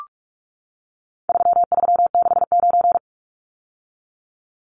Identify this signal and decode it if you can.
Morse code, heard as a keyed tone, about 44 wpm, 700 Hz. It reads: 3369
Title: Numbers - 4-Digit 45wpm